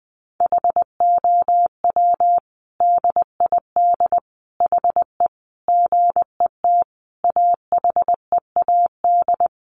Morse code, heard as a keyed tone, about 20 wpm, 700 Hz. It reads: HOW DID HE GET AHEAD
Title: Sentences from Top 1100 Words 20wpm